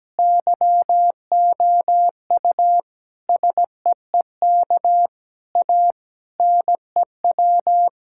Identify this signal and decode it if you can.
Morse code, heard as a keyed tone, about 17 wpm, 700 Hz. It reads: YOU SEEK A NEW
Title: Sentences from Top 1600 Words 17wpm